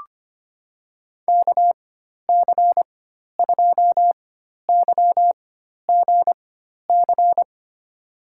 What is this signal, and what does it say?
Morse code, heard as a keyed tone, about 25 wpm, spaced as for 12 wpm, 700 Hz. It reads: KC2YGC
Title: USA Call Signs - 6 Characters - 4x Character Spacing - 25wpm